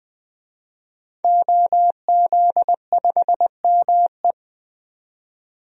Morse code, heard as a keyed tone, about 20 wpm, 700 Hz. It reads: OZ5ME
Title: DXCC-100 Call Signs - 5 Characters - 20wpm